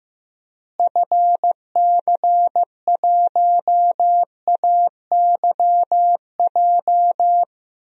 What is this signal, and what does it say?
Morse code, heard as a keyed tone, about 15 wpm, 700 Hz. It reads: FC1AYJ
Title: DXCC-100 Call Signs - 6 Characters - Repeated 3x - 15wpm